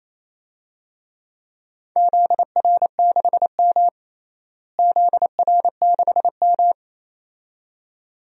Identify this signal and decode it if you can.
Morse code, heard as a keyed tone, about 28 wpm, 700 Hz. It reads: ZR6M ZR6M
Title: DXCC-100 Call Signs - 4 Characters - Repeated 2x - 28wpm